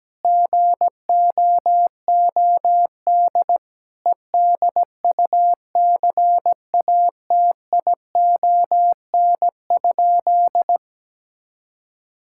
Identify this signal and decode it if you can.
Morse code, heard as a keyed tone, about 17 wpm, 700 Hz. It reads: GOOD EDUCATION?
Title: Sentences from Top 1000 Words - Review Part 6 - 17wpm